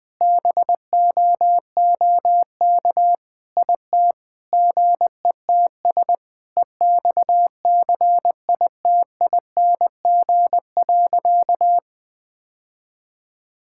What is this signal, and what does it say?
Morse code, heard as a keyed tone, about 21 wpm, 700 Hz. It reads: BOOK IT GETS EXCITING.
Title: Sentences with 5-Word Phrases - Spoken Beginning - 20wpm